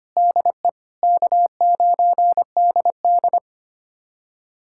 Morse code, heard as a keyed tone, about 25 wpm, 700 Hz. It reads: DE K9DD
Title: USA Callsign DE Callsign - 4 Characters - 25wpm